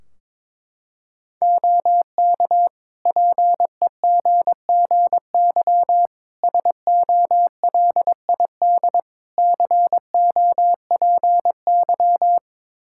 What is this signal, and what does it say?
Morse code, heard as a keyed tone, about 22 wpm, 700 Hz. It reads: OK PEGGY SOLID COPY
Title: Practice QSO - Exchange 3 - More Info - 22wpm